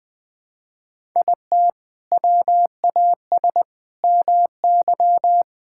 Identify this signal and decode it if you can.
Morse code, heard as a keyed tone, about 20 wpm, 700 Hz. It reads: IT WAS MY
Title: Sentences from Top 1000 Words - Review Part 5 - 20wpm